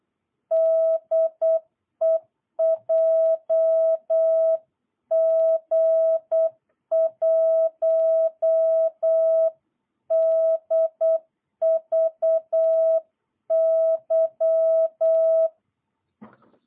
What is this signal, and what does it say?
Morse code, heard as a keyed tone, about 8 wpm, 650 Hz. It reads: DE JG1DVY
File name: jh8cbh / 2022cw